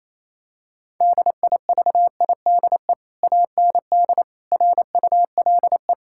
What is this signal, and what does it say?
Morse code, heard as a keyed tone, about 28 wpm, 700 Hz. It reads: DIVIDE AND RULE
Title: Sentences from Top 1600 Words 28wpm